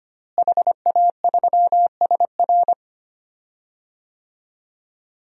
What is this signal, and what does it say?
Morse code, heard as a keyed tone, about 25 wpm, 700 Hz. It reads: HA3SR
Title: DXCC-100 Call Signs - 5 Characters - Repeated 3x - 25wpm